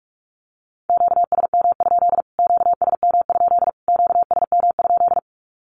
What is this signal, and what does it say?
Morse code, heard as a keyed tone, about 44 wpm, 700 Hz. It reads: QSM? QSM? QSM?
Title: Q-Codes Repeated 3x 45wpm